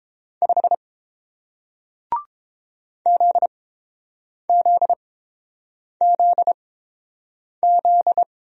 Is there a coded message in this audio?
Yes: Morse code from 0.4 to 8.2 s, heard as a keyed tone, about 30 wpm, spaced as for 7 wpm, 700 Hz. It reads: 5 ZZZZ